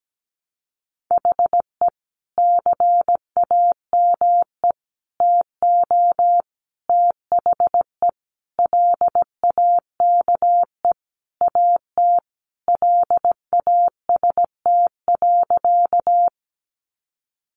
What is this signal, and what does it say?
Morse code, heard as a keyed tone, about 17 wpm, 700 Hz. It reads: HE CAME TO THE LAKE AT LAST.
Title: Sentences from Top 1100 Words 17wpm